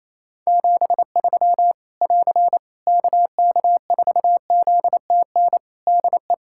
Morse code, heard as a keyed tone, about 28 wpm, 700 Hz. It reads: 73 <AR> KK4ZTN DE
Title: K1USN SST QSO Practice 28wpm